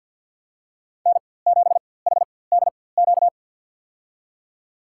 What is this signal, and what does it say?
Morse code, heard as a keyed tone, about 49 wpm, spaced as for 23 wpm, 700 Hz. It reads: N7HDQ
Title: USA Call Signs - 5 Characters - 4x Character Spacing - 50wpm